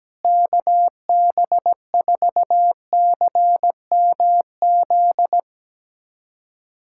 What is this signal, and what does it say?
Morse code, heard as a keyed tone, about 17 wpm, 700 Hz. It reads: KB4CMZ
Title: USA Callsign DE Callsign - 6 Characters - 17wpm